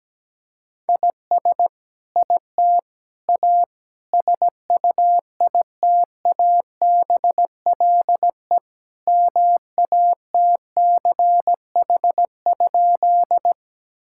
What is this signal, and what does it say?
Morse code, heard as a keyed tone, about 17 wpm, 700 Hz. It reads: IS IT A SUITABLE MATCH?
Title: Sentences from Top 1600 Words 17wpm